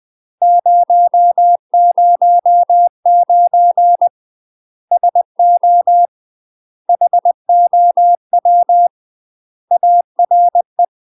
Morse code, heard as a keyed tone, about 20 wpm, 700 Hz. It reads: 009 SO HOW ARE
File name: 20WPM